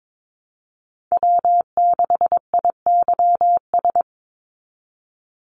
Morse code, heard as a keyed tone, about 22 wpm, 700 Hz. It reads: W6IYS
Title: NA SOTA Chaser Callsigns 22wpm